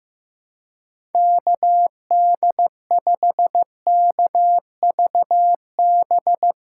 Morse code, heard as a keyed tone, about 16 wpm, 700 Hz. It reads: KD5KVB
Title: USA Callsign DE Callsign - 6 Characters - 15wpm